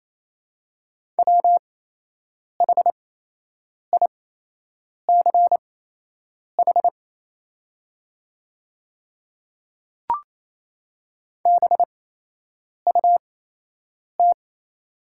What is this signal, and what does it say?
Morse code, heard as a keyed tone, about 28 wpm, spaced as for 8 wpm, 700 Hz. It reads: WHICH BUT
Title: 8x Character Spacing - Top 100 Words - 28wpm